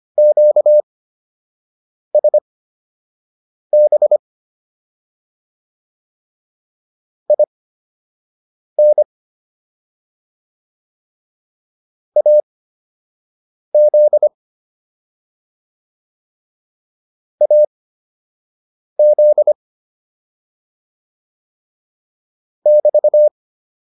Morse code, heard as a keyed tone, about 25 wpm, spaced as for 6 wpm, 600 Hz. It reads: QSB IN AZ AZ <BT>
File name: QSO 204 06